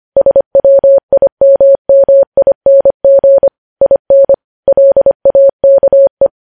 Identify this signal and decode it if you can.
Morse code, heard as a keyed tone, about 25 wpm, 550 Hz. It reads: SWIMMING IN LAKE
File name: podcast / News headlines-1x-25x25